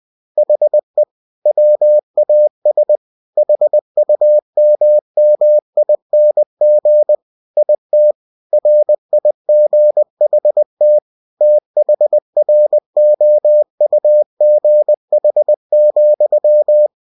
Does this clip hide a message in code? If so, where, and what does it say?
Yes: Morse code from 0.4 to 16.9 s, heard as a keyed tone, about 20 wpm, 600 Hz. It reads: HE WAS HUMMING IT RIGHT THROUGH,